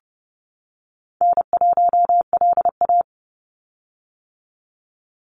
Morse code, heard as a keyed tone, about 30 wpm, 700 Hz. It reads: N1LA
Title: USA Call Signs 30wpm